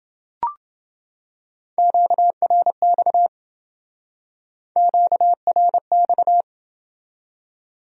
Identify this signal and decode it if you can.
Morse code, heard as a keyed tone, about 28 wpm, 700 Hz. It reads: QRX QRX